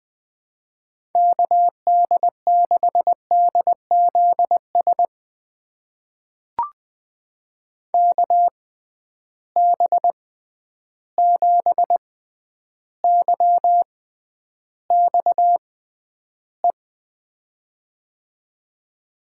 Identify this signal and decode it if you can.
Morse code, heard as a keyed tone, about 20 wpm, spaced as for 10 wpm, 700 Hz. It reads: KD6DZS KB7YXE
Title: USA Call Signs - 6 Characters - 6x Character Spacing - 20wpm